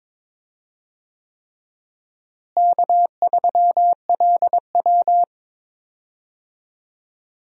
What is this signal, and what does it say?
Morse code, heard as a keyed tone, about 22 wpm, 700 Hz. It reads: K3LW
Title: USA Call Signs - 4 Characters - Repeated 3x - 22wpm